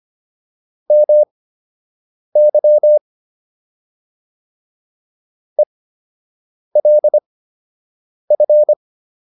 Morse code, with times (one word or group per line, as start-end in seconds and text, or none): 0.9-3.0 MY
5.6-8.7 ELF